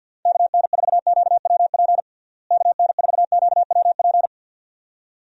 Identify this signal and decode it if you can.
Morse code, heard as a keyed tone, about 50 wpm, 700 Hz. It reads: KN4QWP KN4QWP
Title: USA Call Signs - 6 Characters - Repeated 3x - 50wpm